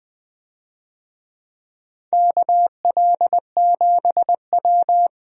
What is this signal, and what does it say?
Morse code, heard as a keyed tone, about 20 wpm, 700 Hz. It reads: KL7W
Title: DXCC-100 Call Signs - 4 Characters - Repeated 3x - 20wpm